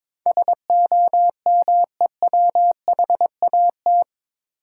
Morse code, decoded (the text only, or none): SOMEWHAT